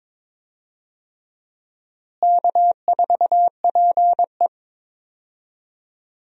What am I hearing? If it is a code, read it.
Morse code, heard as a keyed tone, about 22 wpm, 700 Hz. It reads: K4PE
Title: USA Call Signs - 4 Characters - Repeated 3x - 22wpm